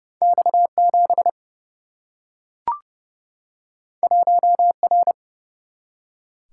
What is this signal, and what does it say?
Morse code, heard as a keyed tone, about 30 wpm, 700 Hz. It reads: X7 1R
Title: Single-Letter Single-Number 30wpm